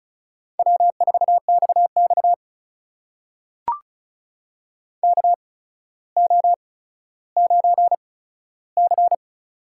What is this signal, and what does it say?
Morse code, heard as a keyed tone, about 36 wpm, spaced as for 15 wpm, 700 Hz. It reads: W4XX KO9C
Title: USA Call Signs - 4 Characters - 8x Character Spacing - 35wpm